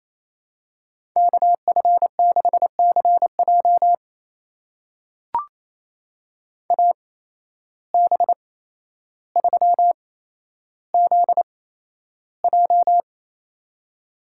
Morse code, heard as a keyed tone, about 28 wpm, spaced as for 12 wpm, 700 Hz. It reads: KF6CJ AB3ZJ
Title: USA Call Signs - 5 Characters - 8x Character Spacing - 28wpm